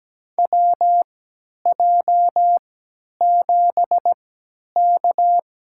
Morse code, heard as a keyed tone, about 17 wpm, spaced as for 10 wpm, 700 Hz. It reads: WJ7K
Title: USA Call Signs - 4 Characters - 3x Character Spacing - 17wpm